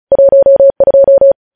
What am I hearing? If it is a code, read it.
Morse code, heard as a keyed tone, about 36 wpm, 550 Hz. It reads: 12
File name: podcast / News headlines-1x-35x35